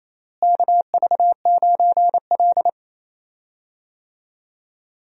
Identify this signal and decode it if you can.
Morse code, heard as a keyed tone, about 28 wpm, 700 Hz. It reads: KV9L
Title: USA Call Signs - 4 Characters - Repeated 2x - 28wpm